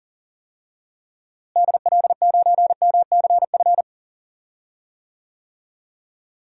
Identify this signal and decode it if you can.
Morse code, heard as a keyed tone, about 38 wpm, 700 Hz. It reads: DL9MCF
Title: DXCC-100 Call Signs - 6 Characters - Repeated 3x - 40wpm